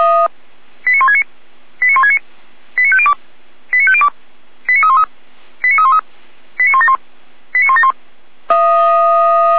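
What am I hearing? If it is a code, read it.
Five-tone selective call (ZVEI), heard as quick runs of five tones, tones of about 70 ms. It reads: 89169 89169 89502 89502 89313 89313 89171 89171
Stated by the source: ZVEI five-tone selective call 89337, received by SDR, from the feuerwehr-orlamuende.org audio server.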